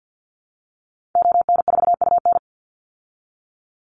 Morse code, heard as a keyed tone, about 50 wpm, 700 Hz. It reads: ON4UN
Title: Call Signs Repeated 3x - 50wpm